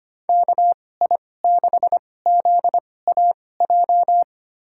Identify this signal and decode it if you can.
Morse code, heard as a keyed tone, about 25 wpm, spaced as for 18 wpm, 700 Hz. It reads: KI6ZAJ
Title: USA Call Signs - 6 Characters - 2x Character Spacing - 25wpm